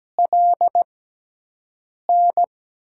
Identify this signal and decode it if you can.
Morse code, heard as a keyed tone, about 17 wpm, spaced as for 6 wpm, 700 Hz. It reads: LN
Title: USA Call Signs - 5 Characters - 6x Character Spacing - 17wpm